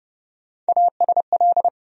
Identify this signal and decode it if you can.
Morse code, heard as a keyed tone, about 30 wpm, 700 Hz. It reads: ASL
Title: USA Call Sign Postfixes - 3 Letters 30wpm